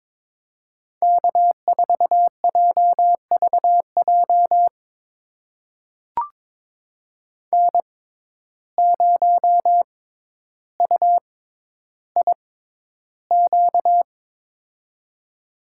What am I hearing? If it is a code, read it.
Morse code, heard as a keyed tone, about 22 wpm, spaced as for 11 wpm, 700 Hz. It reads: K4JVJ N0UIQ
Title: USA Call Signs - 5 Characters - 6x Character Spacing - 22wpm